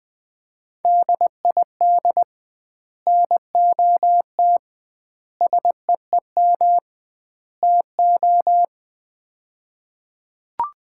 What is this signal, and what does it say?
Morse code, heard as a keyed tone, about 20 wpm, 700 Hz. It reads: DID NOT SEEM TO
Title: 2x Word-Spacing - Sets of 4 Words - 20wpm